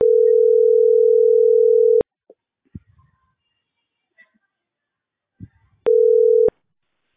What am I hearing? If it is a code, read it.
Ringback tone (North America), 440 + 480 Hz.